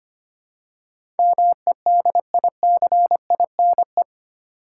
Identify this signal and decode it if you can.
Morse code, heard as a keyed tone, about 25 wpm, 700 Hz. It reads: MEDICINE
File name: 8 Letter Words - First 3 Letters Given - 25wpm